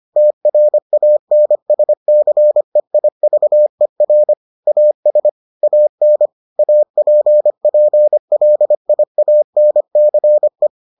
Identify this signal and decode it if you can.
Morse code, heard as a keyed tone, about 25 wpm, 600 Hz. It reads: TRANSCEIVER AS AN APPLIANCE